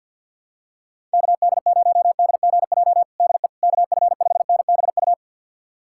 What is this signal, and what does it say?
Morse code, heard as a keyed tone, about 50 wpm, 700 Hz. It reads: KD0DGJ DE KF5NBU
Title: USA Callsign DE Callsign - 6 Characters - 50wpm